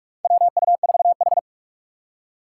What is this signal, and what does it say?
Morse code, heard as a keyed tone, about 44 wpm, 700 Hz. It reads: WU4H
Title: USA Call Signs - 3x Character Spacing - 45wpm